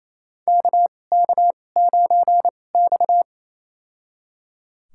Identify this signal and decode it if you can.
Morse code, heard as a keyed tone, about 28 wpm, spaced as for 20 wpm, 700 Hz. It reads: KK9X
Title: USA Call Signs - 4 Characters - 2x Character Spacing - 28wpm